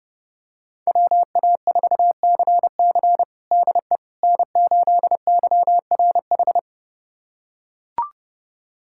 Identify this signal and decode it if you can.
Morse code, heard as a keyed tone, about 30 wpm, 700 Hz. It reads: WA4CC DE N8YRH
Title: USA Callsign DE Callsign - 5 Characters - 30wpm